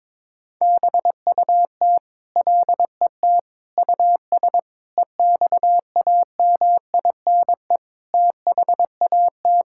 Morse code, heard as a keyed tone, about 22 wpm, 700 Hz. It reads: BUT LET US EXAMINE THAT
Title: Sentences from Top 1600 Words 22wpm